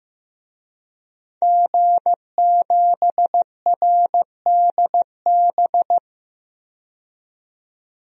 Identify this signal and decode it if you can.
Morse code, heard as a keyed tone, about 15 wpm, 700 Hz. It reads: G7RDB
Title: DXCC-100 Call Signs - 5 Characters - 15wpm